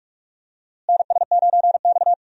Morse code, heard as a keyed tone, about 46 wpm, 700 Hz. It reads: NS9X
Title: USA Call Signs - 4 Characters - Repeated 2x - 45wpm